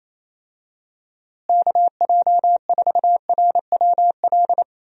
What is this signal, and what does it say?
Morse code, heard as a keyed tone, about 28 wpm, 700 Hz. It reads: KJ4RWL
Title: USA Call Signs - 6 Characters - Repeated 3x - 28wpm